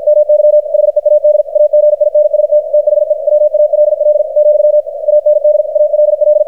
Morse code, heard as a keyed tone, about 52 wpm, 600 Hz, with radio noise. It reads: MY HEAD AGINST THE WALL , AND ALL
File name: archived / CRSnet